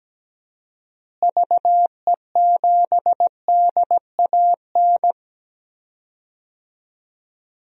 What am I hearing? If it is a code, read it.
Morse code, heard as a keyed tone, about 17 wpm, 700 Hz. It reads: VE7DAN
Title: DXCC-100 Call Signs - 6 Characters - Repeated 2x - 17wpm